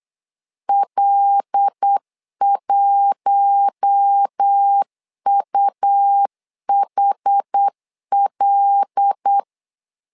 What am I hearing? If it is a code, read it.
Morse code, heard as a keyed tone, about 8 wpm, 800 Hz. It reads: L1UHL
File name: LEZIONI / 5 LBVC / 5 L 8wpm 2831